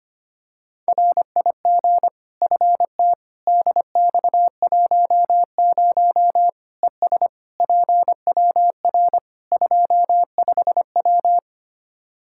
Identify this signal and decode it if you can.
Morse code, heard as a keyed tone, about 25 wpm, 700 Hz. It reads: RIG FT DX10 ES PWR 25W
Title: Practice QSO - Exchange 2 - Info - 25wpm